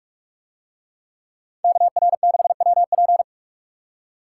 Morse code, heard as a keyed tone, about 44 wpm, 700 Hz. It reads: KR6WP